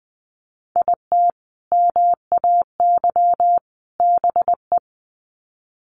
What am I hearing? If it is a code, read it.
Morse code, heard as a keyed tone, about 20 wpm, 700 Hz. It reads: IT MAY BE
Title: Sets of 3 Words 20wpm